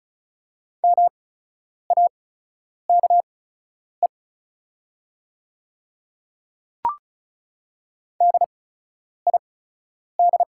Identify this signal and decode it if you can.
Morse code, heard as a keyed tone, about 36 wpm, spaced as for 10 wpm, 700 Hz. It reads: MAKE DID
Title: 8x Character Spacing - Top 200 Words - 35wpm